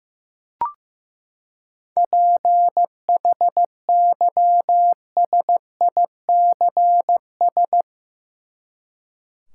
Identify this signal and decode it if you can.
Morse code, heard as a keyed tone, about 15 wpm, 700 Hz. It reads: PHYSICS
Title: Seven Letter Words - 15wpm